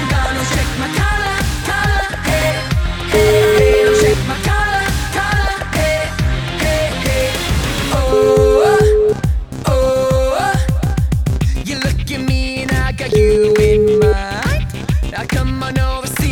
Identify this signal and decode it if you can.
Ringback tone (Europe, 425 Hz), with noise.